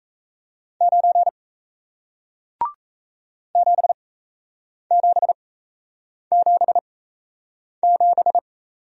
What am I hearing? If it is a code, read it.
Morse code, heard as a keyed tone, about 38 wpm, spaced as for 8 wpm, 700 Hz. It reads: 9 7777